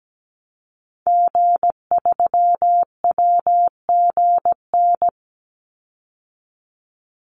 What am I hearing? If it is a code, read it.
Morse code, heard as a keyed tone, about 17 wpm, 700 Hz. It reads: G3WGN
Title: Call Signs - 17wpm